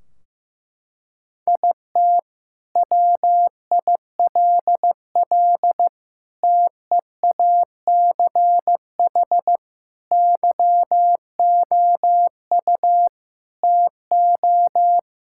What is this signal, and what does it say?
Morse code, heard as a keyed tone, about 14 wpm, 700 Hz. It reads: IT WILL TEACH YOU TO
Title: Sentences from Top 1600 Words 15wpm